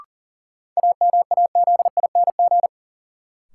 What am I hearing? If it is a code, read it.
Morse code, heard as a keyed tone, about 40 wpm, 700 Hz. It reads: AMAZING
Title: Sentences from Top 1500 Words - New Word Drill - 40wpm